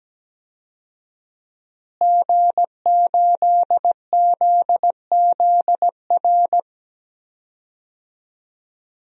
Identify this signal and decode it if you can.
Morse code, heard as a keyed tone, about 17 wpm, 700 Hz. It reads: G8ZZR
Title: DXCC-100 Call Signs - 5 Characters - Repeated 3x - 17wpm